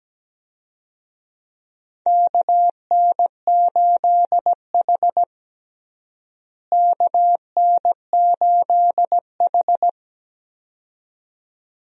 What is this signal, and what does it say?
Morse code, heard as a keyed tone, about 17 wpm, 700 Hz. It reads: KN8H KN8H
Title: USA Call Signs - 4 Characters - Repeated 2x - 17wpm